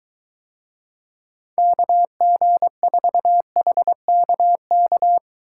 Morse code, heard as a keyed tone, about 22 wpm, 700 Hz. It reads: KG4HKK